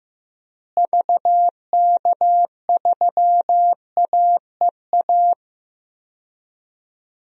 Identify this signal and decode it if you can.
Morse code, heard as a keyed tone, about 15 wpm, 700 Hz. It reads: VK3AEA
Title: DXCC-100 Call Signs - 6 Characters - Repeated 2x - 15wpm